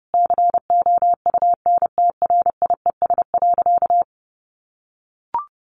Morse code, heard as a keyed tone, about 30 wpm, 700 Hz. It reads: COUNTRIES.
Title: A Sound Of Thunder 30wpm